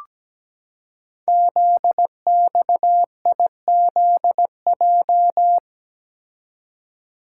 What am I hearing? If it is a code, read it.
Morse code, heard as a keyed tone, about 17 wpm, 700 Hz. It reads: ZXIZJ
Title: Code Groups - 5 Letters - 17wpm